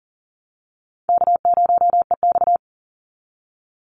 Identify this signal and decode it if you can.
Morse code, heard as a keyed tone, about 40 wpm, 700 Hz. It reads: K0EX
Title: Call Signs - 40wpm